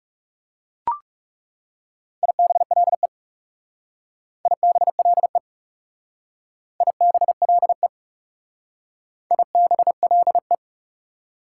Morse code, heard as a keyed tone, about 38 wpm, 690 Hz. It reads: IBLE IBLE IBLE IBLE